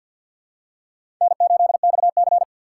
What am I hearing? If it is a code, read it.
Morse code, heard as a keyed tone, about 52 wpm, 700 Hz. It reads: N8XC
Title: USA Call Signs - 3x Character Spacing - 50wpm